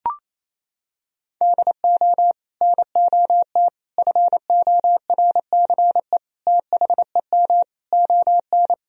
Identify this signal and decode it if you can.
Morse code, heard as a keyed tone, about 28 wpm, 700 Hz. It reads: DO NOT FORCE THEM ON
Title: Sentences from Top 1000 Words - Review Part 3 - 28wpm